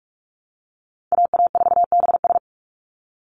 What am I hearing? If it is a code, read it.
Morse code, heard as a keyed tone, about 44 wpm, 700 Hz. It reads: AA4BS
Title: USA Call Signs 45wpm